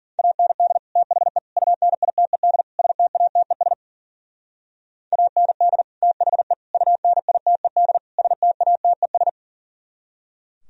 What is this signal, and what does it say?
Morse code, heard as a keyed tone, about 49 wpm, 700 Hz. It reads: AND THE UNITED STATES AND THE UNITED STATES
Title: Speed-Racing - Sets of 4 Words 40wpm